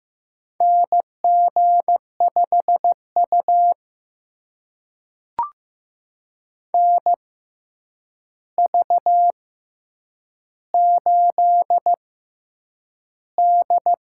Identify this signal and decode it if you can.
Morse code, heard as a keyed tone, about 15 wpm, spaced as for 8 wpm, 700 Hz. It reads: NG5U NV8D
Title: USA Call Signs - 4 Characters - 6x Character Spacing - 15wpm